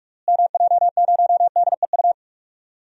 Morse code, heard as a keyed tone, about 46 wpm, 700 Hz. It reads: MJ0DEU
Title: DXCC-100 Call Signs - 6 Characters - Repeated 2x - 45wpm